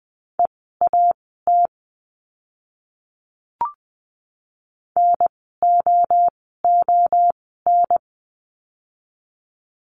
Morse code, heard as a keyed tone, about 20 wpm, spaced as for 14 wpm, 700 Hz. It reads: EAT NOON
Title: 024 Lesson 5f - Letter o - Words 20wpm